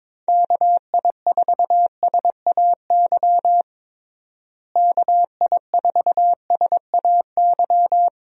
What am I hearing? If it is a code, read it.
Morse code, heard as a keyed tone, about 23 wpm, 700 Hz. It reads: KI4SAY KI4SAY
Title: USA Call Signs - 6 Characters - Repeated 2x - 22wpm